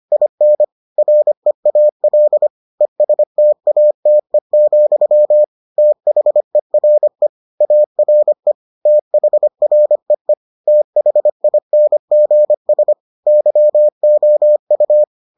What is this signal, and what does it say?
Morse code, heard as a keyed tone, about 25 wpm, 600 Hz. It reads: IN REAL ESTATE, THERE ARE THREE THINGS YOU